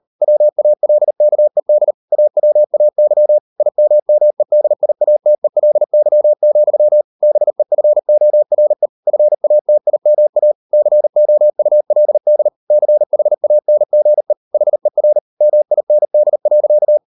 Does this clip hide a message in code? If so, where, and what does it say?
Yes: Morse code from 0.2 to 17.0 s, heard as a keyed tone, about 38 wpm, 600 Hz. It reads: WALKED AWAY IMMEDIATELY, BEFORE FATIMA COULD CHANGE HER MIND.